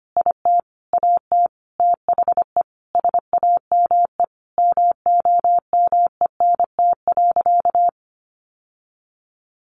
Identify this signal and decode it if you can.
Morse code, heard as a keyed tone, about 25 wpm, 700 Hz. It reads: IT AT THE SAME MOMENT.
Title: Sentences from Top 1000 Words 25wpm